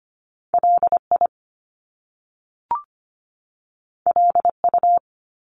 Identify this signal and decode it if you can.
Morse code, heard as a keyed tone, about 25 wpm, 700 Hz. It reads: LI LU
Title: Top Two Letter Combinations in Words 25wpm